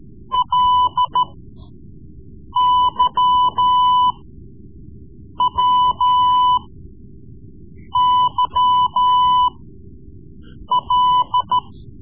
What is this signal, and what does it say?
Morse code, heard as a keyed tone, about 14 wpm, spaced as for 5 wpm, 1000 Hz. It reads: LYWYL
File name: Morse Lessons / Morse Lessons / Tape-3B